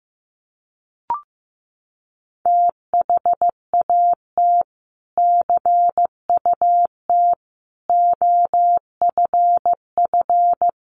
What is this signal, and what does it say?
Morse code, heard as a keyed tone, about 14 wpm, 700 Hz. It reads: THAT CUT OFF
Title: Sentences from Top 1100 Words 15wpm